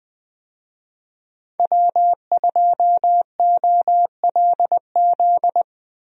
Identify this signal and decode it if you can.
Morse code, heard as a keyed tone, about 20 wpm, 700 Hz. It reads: W2OLZ
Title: USA Call Signs - 5 Characters - Repeated 3x - 20wpm